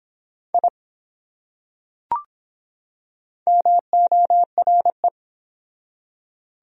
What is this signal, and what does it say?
Morse code, heard as a keyed tone, about 22 wpm, 700 Hz. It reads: I MORE